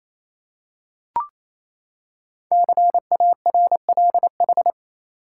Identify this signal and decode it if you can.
Morse code, heard as a keyed tone, about 29 wpm, 700 Hz. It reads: CARLH
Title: Code Groups - 5 Letters - 28wpm